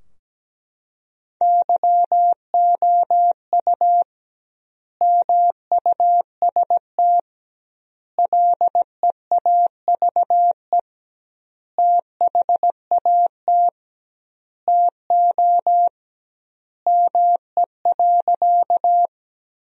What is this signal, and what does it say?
Morse code, heard as a keyed tone, about 17 wpm, 700 Hz. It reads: YOU MUST LEAVE THAT TO ME.
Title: 2x Word-Spacing - Sentences from Top 400 Words - 17wpm